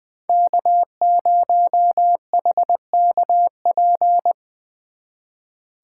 Morse code, heard as a keyed tone, about 23 wpm, 700 Hz. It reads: K0HKP
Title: Speed Racing - USA Call Signs 20wpm